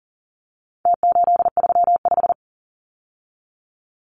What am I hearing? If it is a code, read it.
Morse code, heard as a keyed tone, about 40 wpm, 700 Hz. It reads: T835
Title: Numbers - Contest Style - 4-Digit 40wpm